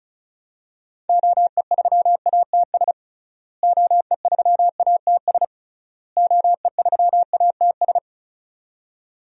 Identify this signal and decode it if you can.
Morse code, heard as a keyed tone, about 34 wpm, 700 Hz. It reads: OE3ATS OE3ATS OE3ATS
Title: DXCC-100 Call Signs - 6 Characters - Repeated 3x - 35wpm